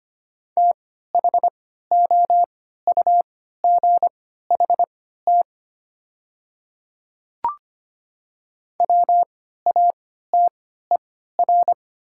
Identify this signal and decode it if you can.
Morse code, heard as a keyed tone, about 25 wpm, spaced as for 14 wpm, 700 Hz. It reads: THOUGHT WATER
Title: 3x Character Spacing - Top 300 Words - 25wpm